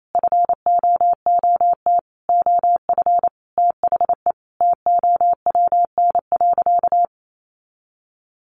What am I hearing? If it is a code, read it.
Morse code, heard as a keyed tone, about 28 wpm, 700 Hz. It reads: FOOT OF THE TOWN.
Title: Sentences from Top 1400 Words 28wpm